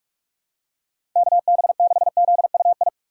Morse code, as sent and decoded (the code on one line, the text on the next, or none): -.- -... -.... --.. ..- ..
KB6ZUI